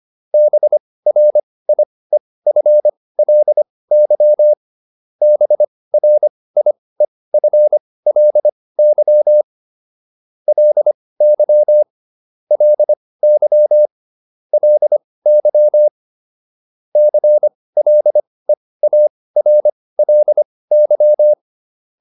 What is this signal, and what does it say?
Morse code, heard as a keyed tone, about 25 wpm, spaced as for 18 wpm, 600 Hz. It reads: BRIEFLY BRIEFLY LY LY LY CLEARLY